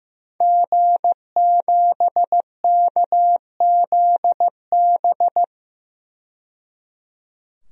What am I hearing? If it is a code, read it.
Morse code, heard as a keyed tone, about 16 wpm, 700 Hz. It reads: G7KZB
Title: DXCC-100 Call Signs - 5 Characters - 15wpm